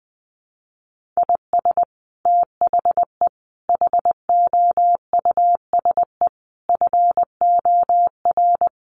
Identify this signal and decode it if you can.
Morse code, heard as a keyed tone, about 20 wpm, 700 Hz. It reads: IS THE HOUSE FOR
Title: Sentences from Top 1400 Words 20wpm